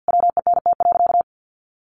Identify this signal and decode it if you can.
Morse code, heard as a keyed tone, about 50 wpm, 700 Hz. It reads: WENT.
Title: Sentences from Top 1500 Words 50wpm